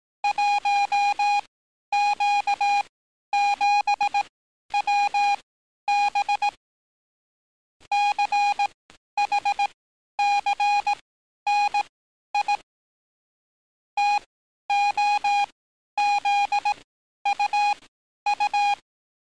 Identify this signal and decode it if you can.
Morse code, heard as a keyed tone, about 18 wpm, spaced as for 11 wpm, 820 Hz. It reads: 1Q7WB CHCNI TOZUU
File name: training / NAVPERS 11049to11060 / NAVPERS 11050 RA RB